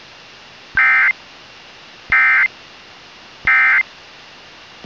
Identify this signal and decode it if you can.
Emergency Alert System (SAME) tones, heard as harsh buzzing bursts. It is heard from 0.7 to 3.9 s.